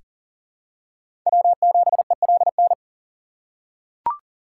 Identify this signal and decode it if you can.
Morse code, heard as a keyed tone, about 40 wpm, 700 Hz. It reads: W7ELN
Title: USA Call Signs - 5 Characters - 4x Character Spacing - 40wpm